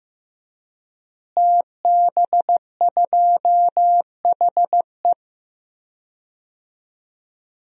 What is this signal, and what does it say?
Morse code, heard as a keyed tone, about 15 wpm, 700 Hz. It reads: TB2HE
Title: DXCC-100 Call Signs - 5 Characters - Repeated 3x - 15wpm